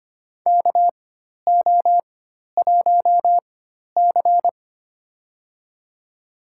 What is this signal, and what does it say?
Morse code, heard as a keyed tone, about 25 wpm, spaced as for 12 wpm, 700 Hz. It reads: KO1C
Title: USA Call Signs - 4 Characters - 4x Character Spacing - 25wpm